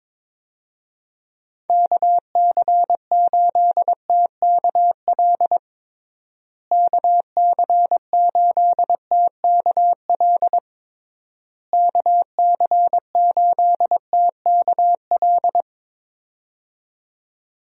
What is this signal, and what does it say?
Morse code, heard as a keyed tone, about 22 wpm, 700 Hz. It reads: KC8TKL KC8TKL KC8TKL
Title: USA Call Signs - 6 Characters - Repeated 3x - 22wpm